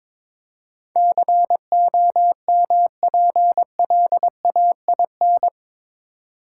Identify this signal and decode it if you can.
Morse code, heard as a keyed tone, about 22 wpm, 700 Hz. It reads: COMPLAIN
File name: practice / 22wpm / Eight Letter Words - 22wpm